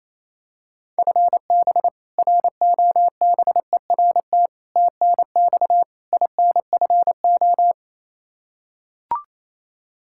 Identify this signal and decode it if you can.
Morse code, heard as a keyed tone, about 28 wpm, 700 Hz. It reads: FB ROBERT TNX INFO
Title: K1USN SST QSO Practice 28wpm